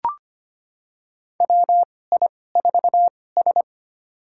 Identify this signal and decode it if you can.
Morse code, heard as a keyed tone, about 25 wpm, spaced as for 18 wpm, 700 Hz. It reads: WI4S
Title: USA Call Signs - 4 Characters - 2x Character Spacing - 25wpm